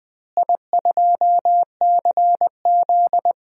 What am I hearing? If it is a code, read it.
Morse code, heard as a keyed tone, about 21 wpm, 700 Hz. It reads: I2CZ
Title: DXCC-100 Call Signs - 4 Characters - Repeated 2x - 20wpm